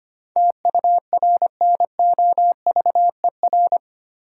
Morse code, heard as a keyed tone, about 25 wpm, 700 Hz. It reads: TURNOVER
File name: 8 Letter Words - First 3 Letters Given - 25wpm